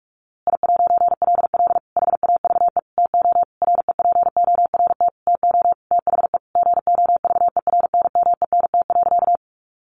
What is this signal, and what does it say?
Morse code, heard as a keyed tone, about 44 wpm, 700 Hz. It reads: I'LL HAVE TO REPORT TO THE GOVERNMENT.
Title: A Sound Of Thunder 45wpm